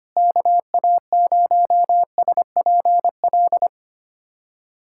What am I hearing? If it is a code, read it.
Morse code, heard as a keyed tone, about 25 wpm, 700 Hz. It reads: KA0SPL
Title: USA Callsign DE Callsign - 6 Characters - 25wpm